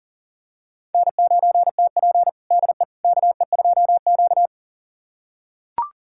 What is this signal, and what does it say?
Morse code, heard as a keyed tone, about 40 wpm, 700 Hz. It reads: N9TP DE KE2Q
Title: USA Callsign DE Callsign - 4 Characters - 40wpm